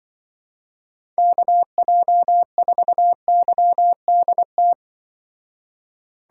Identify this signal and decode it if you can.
Morse code, heard as a keyed tone, about 23 wpm, 700 Hz. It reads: KJ4YDT
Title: Speed Racing - USA 2x3 Call Signs 20wpm